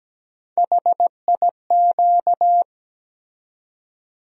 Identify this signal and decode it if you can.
Morse code, heard as a keyed tone, about 17 wpm, 700 Hz. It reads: HIQ
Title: USA Call Sign Postfixes - 3 Letters 17wpm